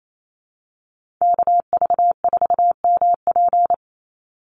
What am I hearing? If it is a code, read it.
Morse code, heard as a keyed tone, about 28 wpm, 700 Hz. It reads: KV4MP
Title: USA Call Signs 28wpm